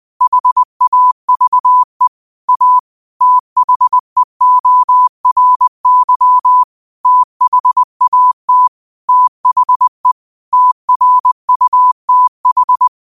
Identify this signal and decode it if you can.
Morse code, heard as a keyed tone, about 20 wpm, 1000 Hz. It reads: HAVE A THEORY THAT THE TRUTH
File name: qotdfiles / 20wpm